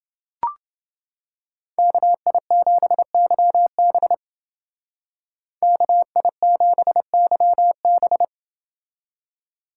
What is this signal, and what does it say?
Morse code, heard as a keyed tone, about 29 wpm, 700 Hz. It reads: KI7YB KI7YB